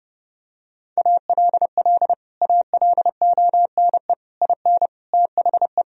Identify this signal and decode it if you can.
Morse code, heard as a keyed tone, about 30 wpm, 700 Hz. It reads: ALL ALONE IN THE
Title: Sentences from Top 1000 Words - Review Part 2 - 30wpm